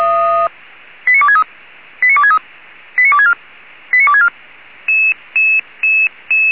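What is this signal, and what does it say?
Five-tone selective call (ZVEI), heard as quick runs of five tones, tones of about 70 ms. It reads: 89263 89263 89264 89264
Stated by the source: Five-tone ZVEI selective call received by SDR.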